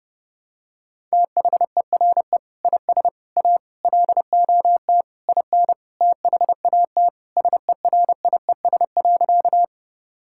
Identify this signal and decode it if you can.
Morse code, heard as a keyed tone, about 30 wpm, 700 Hz. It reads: THERE IS A LOT IN THAT SERIES.
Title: Sentences from Top 1000 Words - Review Part 5 - 30wpm